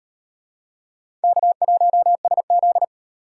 Morse code, heard as a keyed tone, about 38 wpm, 700 Hz. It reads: K1SZ